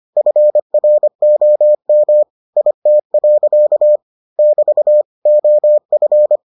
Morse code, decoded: FROM IT. <BT> OF

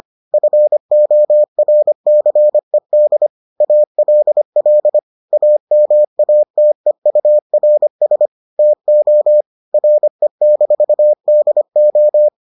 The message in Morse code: FORCED ALL AMATEURS TO RE-DO

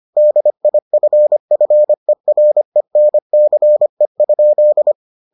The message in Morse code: DIFFERENCE?

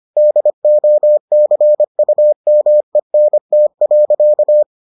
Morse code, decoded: DOCUMENT.

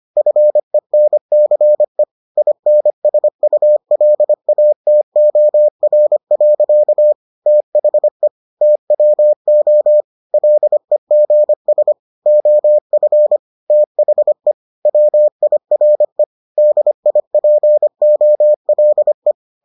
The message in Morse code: FENCE INSULATOR. THE TWO LEGS OF THE WIRE DIPOLE

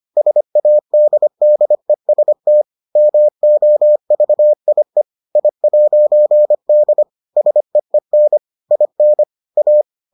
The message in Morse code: SADDEST MOVIE I'D SEEN IN A

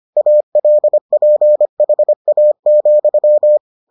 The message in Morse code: ALPHA,

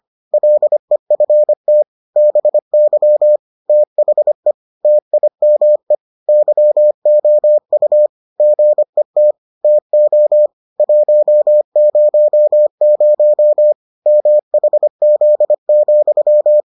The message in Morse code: LEFT BY THE TIME YOU GET TO 100 MHZ,

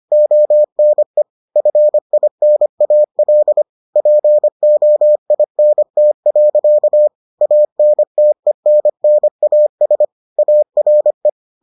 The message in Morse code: ONE FINAL POINT. ANTENNAS ARE

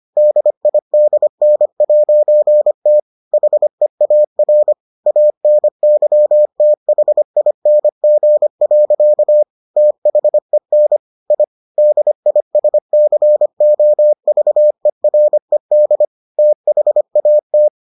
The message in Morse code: DIDN'T HEAR ANYTHING. THEN I DISCOVERED THAT

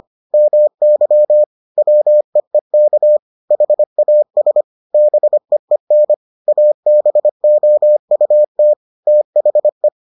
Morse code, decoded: MY WEEK HAS BEEN ABOUT THE

